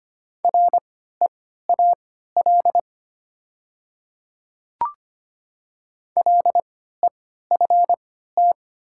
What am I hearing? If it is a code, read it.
Morse code, heard as a keyed tone, about 25 wpm, spaced as for 14 wpm, 700 Hz. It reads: REAL LEFT
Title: 3x Character Spacing - Top 300 Words - 25wpm